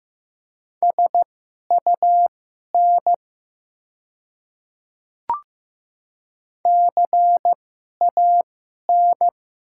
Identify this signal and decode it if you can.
Morse code, heard as a keyed tone, about 15 wpm, spaced as for 11 wpm, 690 Hz. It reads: SUN CAN
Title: 2x Character Spacing - Top 500 Words - 15wpm